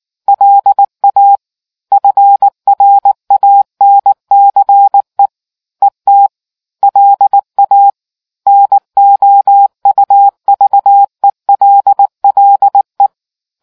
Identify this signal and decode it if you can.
Morse code, heard as a keyed tone, about 19 wpm, 800 Hz. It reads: LA FRANCE ET LA NOUVELLE